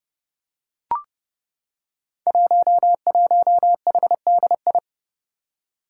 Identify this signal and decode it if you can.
Morse code, heard as a keyed tone, about 30 wpm, 700 Hz. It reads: 11HDI
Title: Five Challenging Letter-Number Combinations 30wpm